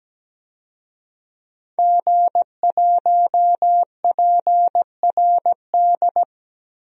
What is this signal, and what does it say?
Morse code, heard as a keyed tone, about 17 wpm, 700 Hz. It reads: G1PRD
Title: DXCC-100 Call Signs - 5 Characters - Repeated 2x - 17wpm